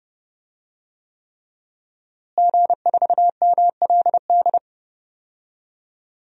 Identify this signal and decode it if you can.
Morse code, heard as a keyed tone, about 29 wpm, 700 Hz. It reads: G4MLD
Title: DXCC-100 Call Signs - 5 Characters - Repeated 3x - 30wpm